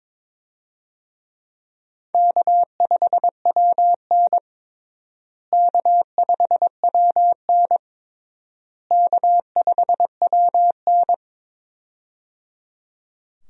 Morse code, heard as a keyed tone, about 22 wpm, 700 Hz. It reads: K5WN K5WN K5WN
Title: USA Call Signs - 4 Characters - Repeated 3x - 22wpm